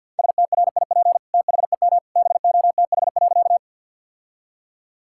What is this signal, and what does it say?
Morse code, heard as a keyed tone, about 52 wpm, 700 Hz. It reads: STRIP THEM BOTH.
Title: Sentences from Top 1600 Words 50wpm